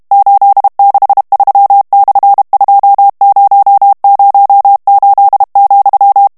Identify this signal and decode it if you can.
Morse code, heard as a keyed tone, about 32 wpm, 800 Hz. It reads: 863/2008,
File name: ESPAGNOL TEXTE EN CLAIR 2 VITESSE 30 MOTS MINUTES